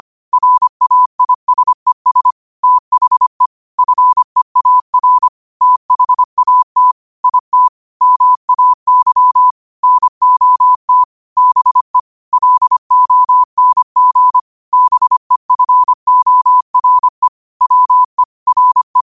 Morse code, heard as a keyed tone, about 25 wpm, 1000 Hz. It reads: RAISES THE FEAR THAT IT MAY NOT BE LONG BEFORE WE RE
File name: qotdfiles / 25wpm